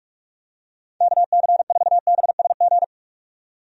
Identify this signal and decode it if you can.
Morse code, heard as a keyed tone, about 41 wpm, 700 Hz. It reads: KC4BSG
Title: USA Call Signs - 6 Characters - 45wpm